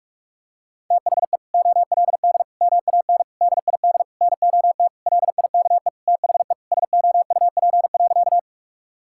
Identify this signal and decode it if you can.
Morse code, heard as a keyed tone, about 46 wpm, 700 Hz. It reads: THE OLD MAN DID NOT LIKE THE SOUP.
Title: Sentences from Top 1000 Words - Review Part 6 - 45wpm